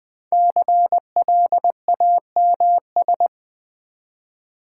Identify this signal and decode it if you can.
Morse code, heard as a keyed tone, about 20 wpm, 700 Hz. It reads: CLAMS
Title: Difficult Words with Hints - 20wpm